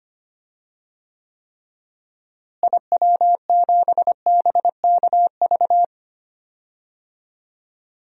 Morse code, heard as a keyed tone, about 25 wpm, 700 Hz. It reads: IW7BKV
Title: DXCC-100 Call Signs - 6 Characters - Repeated 3x - 25wpm